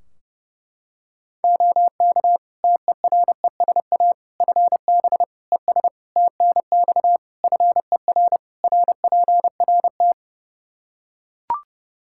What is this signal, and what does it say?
Morse code, heard as a keyed tone, about 32 wpm, 700 Hz. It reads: OK TERESA FB ES TNX FER RPRT
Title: Practice QSO - Exchange 2 - Info - 30wpm